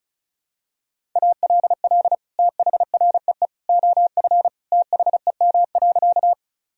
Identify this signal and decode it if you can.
Morse code, heard as a keyed tone, about 34 wpm, 700 Hz. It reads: ALL THREE OF THEM.
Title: Sentences from Top 1000 Words - Review Part 1 - 35wpm